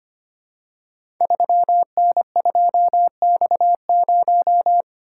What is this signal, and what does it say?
Morse code, heard as a keyed tone, about 25 wpm, 700 Hz. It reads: 3N2X0
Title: Code Groups - 5 Letter:Numbers - 25wpm